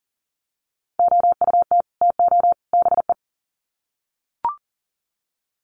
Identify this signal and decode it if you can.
Morse code, heard as a keyed tone, about 40 wpm, 700 Hz. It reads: OUT TO BE
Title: Sets of 3 Words 40wpm